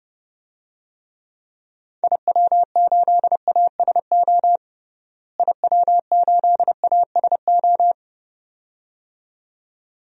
Morse code, heard as a keyed tone, about 30 wpm, 700 Hz. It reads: IW8ASO IW8ASO
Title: DXCC-100 Call Signs - 6 Characters - Repeated 2x - 30wpm